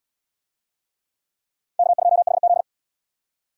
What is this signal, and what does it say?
Morse code, heard as a keyed tone, about 74 wpm, 700 Hz. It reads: K2HZ